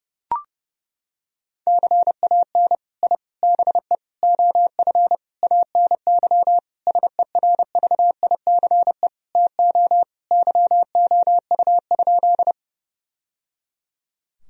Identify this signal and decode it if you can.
Morse code, heard as a keyed tone, about 30 wpm, 700 Hz. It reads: CAN I BE OF ANY SERVICE TO YOU?
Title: Sentences from Top 1000 Words - Review Part 4 - 30wpm